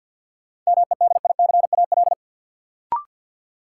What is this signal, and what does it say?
Morse code, heard as a keyed tone, about 50 wpm, 700 Hz. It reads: MEDICAL
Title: Sentences from Top 800 Words - New Word Drill - 50wpm